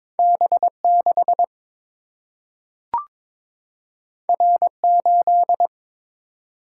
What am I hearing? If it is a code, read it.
Morse code, heard as a keyed tone, about 21 wpm, 700 Hz. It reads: B 6 R 8
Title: Single-Letter Single-Number 22wpm